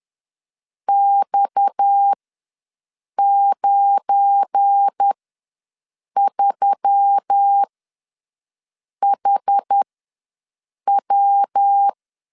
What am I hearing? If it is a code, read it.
Morse code, heard as a keyed tone, about 10 wpm, spaced as for 6 wpm, 800 Hz. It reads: X93HW
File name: LEZIONI / 10 SIGNS / 10A SLASH ws 10wpm 4121